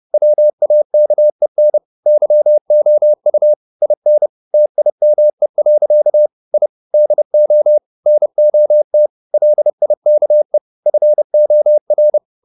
Morse code, heard as a keyed tone, about 30 wpm, 600 Hz. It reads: WAKEN YOU IN TIME. I DO NOT LIKE FOR
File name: the old man and the sea / the old man and the sea 1 30wpm